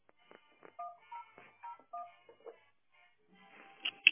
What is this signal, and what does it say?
DTMF, heard as telephone touch tones, tones of about 100 ms. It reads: 1001